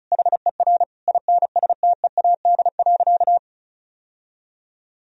Morse code, heard as a keyed tone, about 34 wpm, 700 Hz. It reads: HER INSTEAD.
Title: Sentences from Top 1000 Words - Review Part 4 - 35wpm